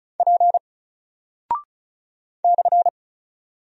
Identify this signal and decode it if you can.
Morse code, heard as a keyed tone, about 36 wpm, 700 Hz. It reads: P /
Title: Single Character (Letter:Number:Punctuation:Prosign) Alt 35wpm